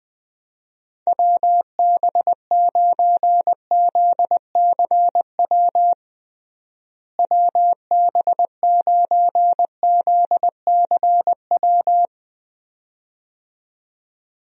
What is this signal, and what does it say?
Morse code, heard as a keyed tone, about 20 wpm, 700 Hz. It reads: WB9ZCW WB9ZCW
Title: USA Call Signs - 6 Characters - Repeated 2x - 20wpm